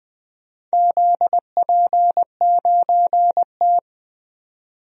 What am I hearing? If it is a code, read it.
Morse code, heard as a keyed tone, about 21 wpm, 700 Hz. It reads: ZP9T
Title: DXCC-100 Call Signs - 4 Characters - 20wpm